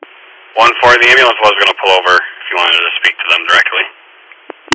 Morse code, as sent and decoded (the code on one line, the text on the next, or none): none